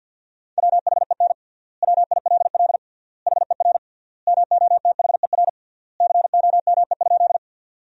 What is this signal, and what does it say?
Morse code, heard as a keyed tone, about 50 wpm, 700 Hz. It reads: WHEN WILL HER MOTHER COME?
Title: 3x Word-Spacing - Sentences from Top 200 Words - 50wpm